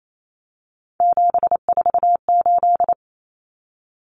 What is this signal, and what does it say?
Morse code, heard as a keyed tone, about 28 wpm, 700 Hz. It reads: 748
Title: Common Cut Numbers - 3-Digit 28wpm